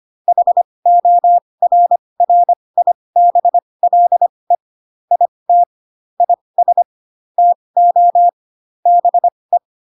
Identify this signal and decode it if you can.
Morse code, heard as a keyed tone, about 25 wpm, spaced as for 20 wpm, 700 Hz. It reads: HORRIBLE IT IS TO BE